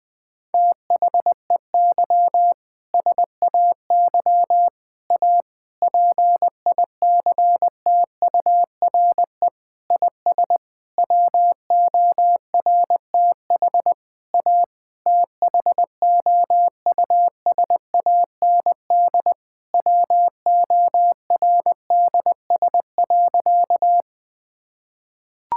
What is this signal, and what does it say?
Morse code, heard as a keyed tone, about 21 wpm, 700 Hz. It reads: THEY SAY A PICTURE IS WORTH A THOUSAND WORDS.
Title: Sentences with 7-Word Phrases - 20wpm